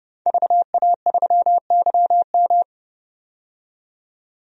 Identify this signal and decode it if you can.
Morse code, heard as a keyed tone, about 31 wpm, 700 Hz. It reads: VA3YM
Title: DXCC-100 Call Signs - 5 Characters - 30wpm